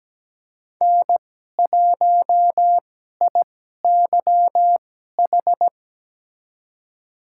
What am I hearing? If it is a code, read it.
Morse code, heard as a keyed tone, about 17 wpm, spaced as for 12 wpm, 700 Hz. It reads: N1IYH
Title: USA Call Signs - 2x Character Spacing - 17wpm